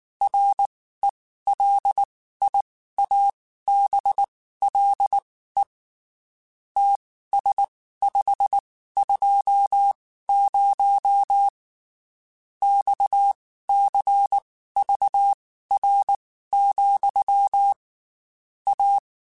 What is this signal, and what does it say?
Morse code, heard as a keyed tone, about 19 wpm, spaced as for 14 wpm, 800 Hz. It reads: RELIABLE TS520 XCVR, A
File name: QSOs / QSOs-mp3s / WD0FBT 1@18